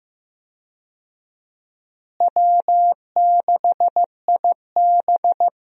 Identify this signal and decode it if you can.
Morse code, heard as a keyed tone, about 15 wpm, 700 Hz. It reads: W6IB
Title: USA Call Signs - 4 Characters - Repeated 3x - 15wpm